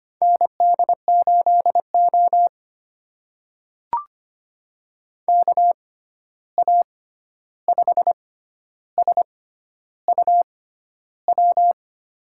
Morse code, heard as a keyed tone, about 25 wpm, spaced as for 11 wpm, 700 Hz. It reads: ND8O KA5SUW
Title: USA Call Signs - 6x Character Spacing - 25wpm